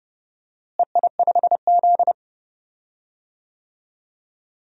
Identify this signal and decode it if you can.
Morse code, heard as a keyed tone, about 32 wpm, 700 Hz. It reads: EI5Z
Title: DXCC-100 Call Signs - 4 Characters - Repeated 3x - 30wpm